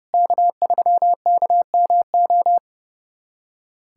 Morse code, heard as a keyed tone, about 30 wpm, 700 Hz. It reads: K3KMO
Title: USA Callsign DE Callsign 30wpm